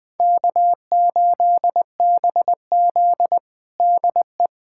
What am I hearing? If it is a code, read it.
Morse code, heard as a keyed tone, about 20 wpm, 700 Hz. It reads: K8BZ DE
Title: Practice QSO - Exchange 4 - Ending - 20wpm